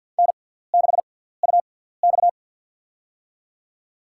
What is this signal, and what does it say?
Morse code, heard as a keyed tone, about 54 wpm, spaced as for 17 wpm, 700 Hz. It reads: N6UX
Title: USA Call Signs - 4 Characters - 6x Character Spacing - 50wpm